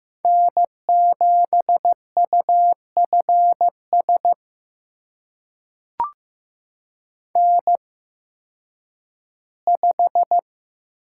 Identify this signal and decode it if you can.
Morse code, heard as a keyed tone, about 15 wpm, spaced as for 10 wpm, 700 Hz. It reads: N7UFS N5
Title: USA Call Signs - 5 Characters - 8x Character Spacing - 15wpm